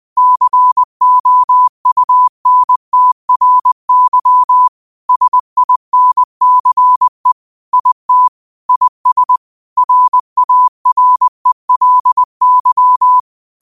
Morse code, heard as a keyed tone, about 20 wpm, 1000 Hz. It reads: COUNTRY SINCE IT IS RARELY